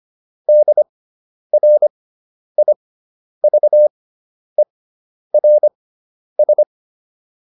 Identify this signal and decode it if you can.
Morse code, heard as a keyed tone, about 25 wpm, spaced as for 10 wpm, 600 Hz. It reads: DRIVERS